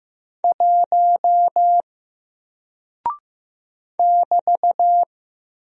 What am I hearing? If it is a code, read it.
Morse code, heard as a keyed tone, about 15 wpm, 700 Hz. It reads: 1 <BT>
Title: Single Character (Letter:Number:Punctuation:Prosign) Alt 15wpm